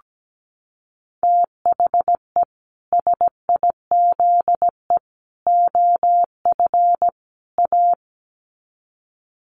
Morse code, heard as a keyed tone, about 17 wpm, 700 Hz. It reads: THE SIZE OF A
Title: Sets of 4 Words 17wpm